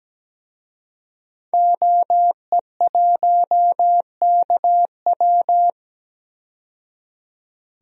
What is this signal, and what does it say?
Morse code, heard as a keyed tone, about 17 wpm, 700 Hz. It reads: OE1KW
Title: DXCC-100 Call Signs - 5 Characters - 17wpm